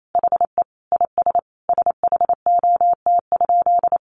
Morse code, heard as a keyed tone, about 28 wpm, 700 Hz. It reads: HE IS SHOT?
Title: Sentences from Top 1100 Words 28wpm